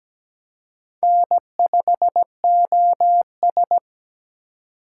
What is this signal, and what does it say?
Morse code, heard as a keyed tone, about 20 wpm, 700 Hz. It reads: N5OS
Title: Speed Racing - USA 1x2 Call Signs 17wpm